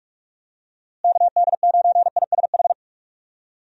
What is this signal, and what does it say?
Morse code, heard as a keyed tone, about 46 wpm, 700 Hz. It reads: KD9ISH
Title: USA Call Signs - 6 Characters - Repeated 3x - 45wpm